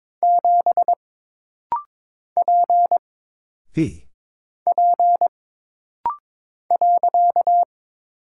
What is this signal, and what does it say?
Morse code, heard as a keyed tone, about 22 wpm, 700 Hz. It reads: 7 P P .
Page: This rapid-fire timing is geared to push your ICR (Instant Character Recognition) skills.